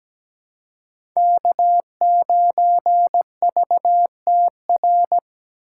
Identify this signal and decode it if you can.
Morse code, heard as a keyed tone, about 17 wpm, 700 Hz. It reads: K9VTR
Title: USA Call Signs - 5 Characters - 4x Character Spacing - 17wpm